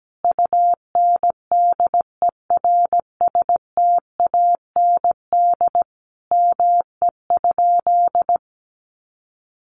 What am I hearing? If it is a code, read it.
Morse code, heard as a keyed tone, about 17 wpm, 700 Hz. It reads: UNDERSTAND ME?
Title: Sentences from Top 600 Words 17wpm